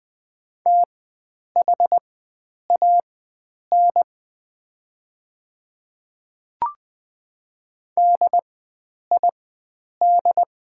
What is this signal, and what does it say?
Morse code, heard as a keyed tone, about 20 wpm, spaced as for 9 wpm, 700 Hz. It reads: THAN DID
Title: 4x Character Spacing - Top 300 Words - 20wpm